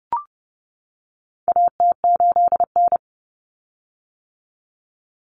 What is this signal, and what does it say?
Morse code, heard as a keyed tone, about 30 wpm, 700 Hz. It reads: AT8N
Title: Common Cut Numbers - 4-Digit 30wpm